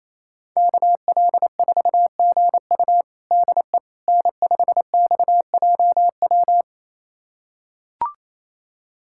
Morse code, heard as a keyed tone, about 28 wpm, 700 Hz. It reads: KL4GU DE N5XJW
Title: USA Callsign DE Callsign - 5 Characters - 28wpm